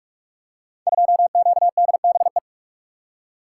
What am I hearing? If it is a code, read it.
Morse code, heard as a keyed tone, about 44 wpm, 700 Hz. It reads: 2QDBE
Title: Code Groups - 5 Letter:Numbers - 45wpm